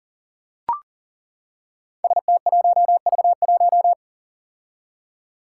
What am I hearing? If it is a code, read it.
Morse code, heard as a keyed tone, about 40 wpm, 700 Hz. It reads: ST1V1
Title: Five Challenging Letter-Number Combinations 40wpm